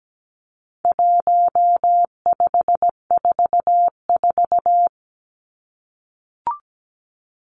Morse code, heard as a keyed tone, about 17 wpm, 700 Hz. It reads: 1544
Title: Numbers - 4-Digit 17wpm